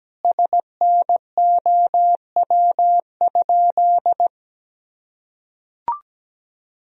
Morse code, heard as a keyed tone, about 17 wpm, 700 Hz. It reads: SNOW?
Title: Sentences from Top 1000 Words - Review Part 3 - 17wpm